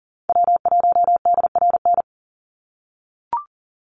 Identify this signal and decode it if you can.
Morse code, heard as a keyed tone, about 40 wpm, 700 Hz. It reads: W1DRN
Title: USA Call Signs 40wpm